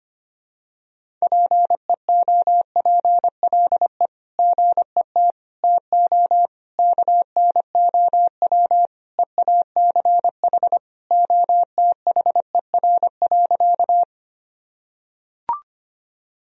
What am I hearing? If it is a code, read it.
Morse code, heard as a keyed tone, about 25 wpm, 700 Hz. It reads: PEOPLE GET TO KNOW EACH OTHER.
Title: Sentences from Top 1000 Words - Review Part 1 - 25wpm